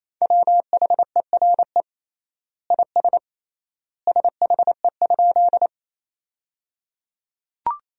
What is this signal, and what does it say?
Morse code, heard as a keyed tone, about 28 wpm, 700 Hz. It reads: WHERE IS SHE?
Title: 3x Word-Spacing - Sentences from Top 200 Words - 28wpm